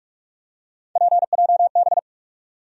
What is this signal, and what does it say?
Morse code, heard as a keyed tone, about 46 wpm, 700 Hz. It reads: PJB
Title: USA Call Sign Postfixes - 3 Letters 45wpm